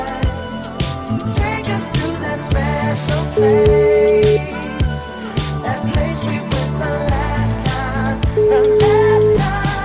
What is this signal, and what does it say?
Ringback tone (Europe, 425 Hz), with noise.